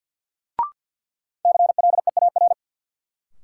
Morse code, heard as a keyed tone, about 46 wpm, 700 Hz. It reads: CLEAR
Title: Sets of 5 Words Encore - New Word Drill - 50wpm